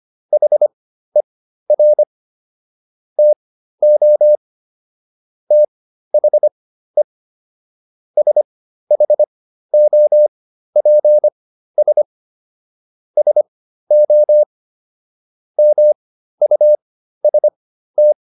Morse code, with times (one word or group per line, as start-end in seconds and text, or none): 0.3-2.0 HER
3.2-4.4 TO
5.5-7.0 THE
8.2-12.0 SHOPS
13.2-14.4 SO
15.6-18.1 MUST